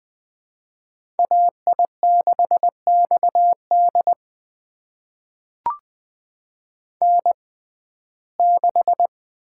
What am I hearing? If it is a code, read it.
Morse code, heard as a keyed tone, about 20 wpm, spaced as for 14 wpm, 700 Hz. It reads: AI6XD N6
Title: USA Call Signs - 5 Characters - 6x Character Spacing - 20wpm